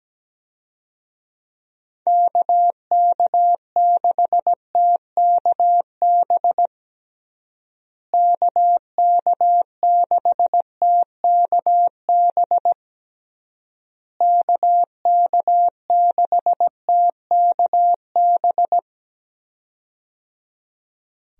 Morse code, heard as a keyed tone, about 17 wpm, 700 Hz. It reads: KK6TKB KK6TKB KK6TKB
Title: USA Call Signs - 6 Characters - Repeated 3x - 17wpm